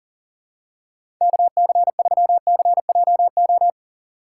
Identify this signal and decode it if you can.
Morse code, heard as a keyed tone, about 40 wpm, 700 Hz. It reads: KC3CJO
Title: USA Call Signs - 6 Characters - 4x Character Spacing - 40wpm